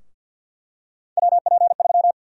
Morse code, heard as a keyed tone, about 49 wpm, 700 Hz. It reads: WP3
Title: USA Call Sign Prefixes - 50wpm